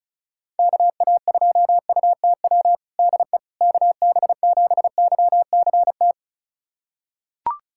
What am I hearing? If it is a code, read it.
Morse code, heard as a keyed tone, about 36 wpm, 700 Hz. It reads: KA2UTW DE KB7YCT
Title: USA Callsign DE Callsign 35wpm